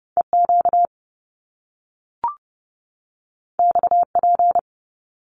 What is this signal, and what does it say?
Morse code, heard as a keyed tone, about 30 wpm, 700 Hz. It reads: EQ XP
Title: Top Two Letter Combinations in Words 30wpm